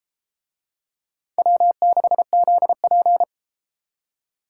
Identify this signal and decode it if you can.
Morse code, heard as a keyed tone, about 31 wpm, 690 Hz. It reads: W6ZP